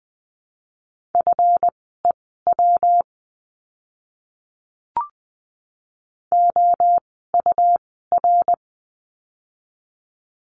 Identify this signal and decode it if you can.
Morse code, heard as a keyed tone, about 20 wpm, spaced as for 15 wpm, 700 Hz. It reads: FEW OUR
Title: 206 Lesson 33f - Letter j - Words 20wpm